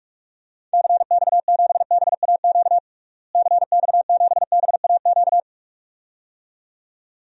Morse code, heard as a keyed tone, about 43 wpm, 700 Hz. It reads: CX7BAQ CX7BAQ
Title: DXCC-100 Call Signs - 6 Characters - Repeated 2x - 45wpm